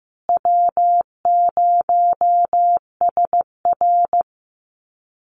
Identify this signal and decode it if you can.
Morse code, heard as a keyed tone, about 15 wpm, 700 Hz. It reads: W0SR
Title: USA Call Signs Repeated 3x 15wpm